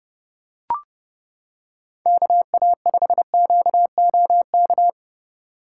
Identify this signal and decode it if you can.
Morse code, heard as a keyed tone, about 29 wpm, 700 Hz. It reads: KA5QOK